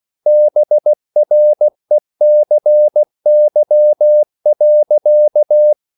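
Morse code, heard as a keyed tone, about 16 wpm, 600 Hz. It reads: BRECY.
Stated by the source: Slow (QRS) Morse code at 16 wpm, from CW med Gnister.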